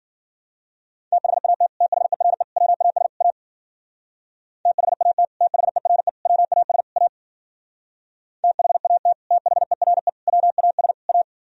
Morse code, heard as a keyed tone, about 49 wpm, 700 Hz. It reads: THAT THERE WAS A THAT THERE WAS A THAT THERE WAS A
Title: Speed-Racing - Sets of 4 Words 40wpm